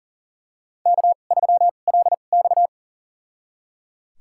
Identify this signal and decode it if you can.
Morse code, heard as a keyed tone, about 39 wpm, spaced as for 29 wpm, 690 Hz. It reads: K3LX
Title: USA Call Signs - 4 Characters - 2x Character Spacing - 40wpm